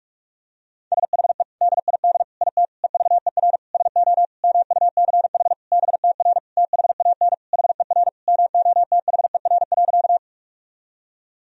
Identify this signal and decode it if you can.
Morse code, heard as a keyed tone, about 46 wpm, 700 Hz. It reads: SHE DID IT EVER SO MUCH BTR THAN HER MOTHER.
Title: Sentences from Top 400 Words and Abbreviations 45wpm